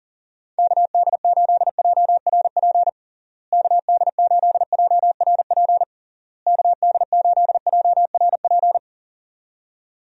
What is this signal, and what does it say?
Morse code, heard as a keyed tone, about 40 wpm, 700 Hz. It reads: KD8JRP KD8JRP KD8JRP
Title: USA Call Signs - 6 Characters - Repeated 3x - 40wpm